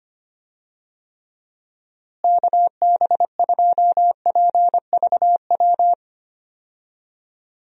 Morse code, heard as a keyed tone, about 25 wpm, 700 Hz. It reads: KB2PVW
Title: USA Call Signs - 6 Characters - Repeated 2x - 25wpm